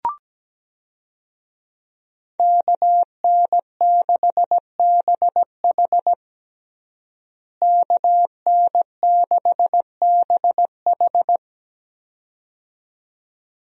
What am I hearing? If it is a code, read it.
Morse code, heard as a keyed tone, about 17 wpm, 700 Hz. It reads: KN6BH KN6BH
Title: USA Call Signs - 5 Characters - Repeated 2x - 17wpm